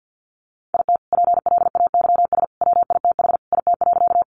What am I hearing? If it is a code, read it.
Morse code, heard as a keyed tone, about 50 wpm, 700 Hz. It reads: IT PLAYS WITH IT.
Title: Additional Easy Phrases 50wpm